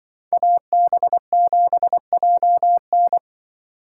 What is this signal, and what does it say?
Morse code, heard as a keyed tone, about 24 wpm, 700 Hz. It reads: AB7JN
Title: Speed Racing - USA 2x2 Call Signs 20wpm